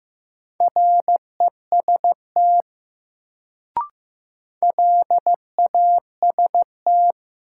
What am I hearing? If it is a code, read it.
Morse code, heard as a keyed tone, about 15 wpm, 700 Hz. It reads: REST LAST
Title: Sets of 2 Words - New Word Drill - 15wpm